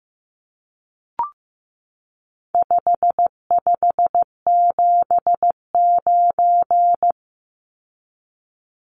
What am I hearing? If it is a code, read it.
Morse code, heard as a keyed tone, about 15 wpm, 700 Hz. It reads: 5579
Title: Numbers - 4-Digit 15wpm